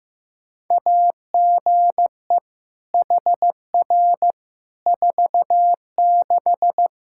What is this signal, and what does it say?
Morse code, heard as a keyed tone, about 16 wpm, 700 Hz. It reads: AGE HR 46
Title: K1USN SST QSO Practice 15wpm